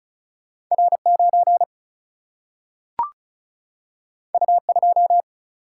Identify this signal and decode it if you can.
Morse code, heard as a keyed tone, about 36 wpm, 700 Hz. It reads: R 9 U 2
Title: Single-Letter Single-Number 35wpm